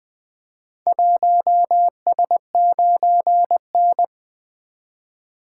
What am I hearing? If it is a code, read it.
Morse code, heard as a keyed tone, about 20 wpm, 700 Hz. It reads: 1S9N
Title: Four Letter-Number Combinations - 20wpm